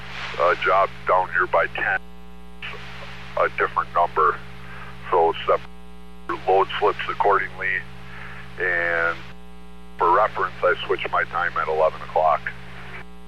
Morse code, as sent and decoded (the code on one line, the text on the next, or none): none